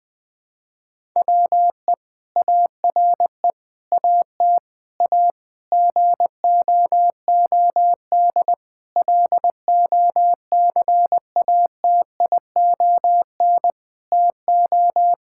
Morse code, heard as a keyed tone, about 21 wpm, 700 Hz. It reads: WE ARE AT A GOOD LOCATION TO
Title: Sentences from Top 1000 Words - Review Part 6 - 20wpm